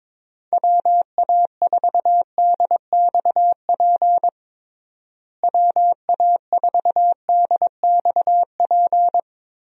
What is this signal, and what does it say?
Morse code, heard as a keyed tone, about 21 wpm, 700 Hz. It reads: WA4DXP WA4DXP
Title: USA Call Signs - 6 Characters - Repeated 3x - 22wpm